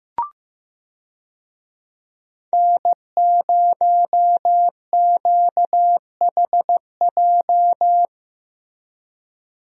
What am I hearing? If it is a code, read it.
Morse code, heard as a keyed tone, about 15 wpm, 700 Hz. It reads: N0QHJ
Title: USA Call Signs - 5 Characters - Repeated 2x - 15wpm